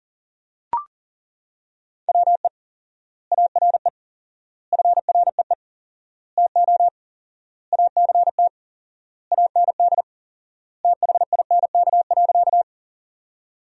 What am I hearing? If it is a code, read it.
Morse code, heard as a keyed tone, about 40 wpm, 700 Hz. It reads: WE ARE FREE TO ACT AND THINK.
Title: 4x Word-Spacing - Sentences from Top 500 Words - 40wpm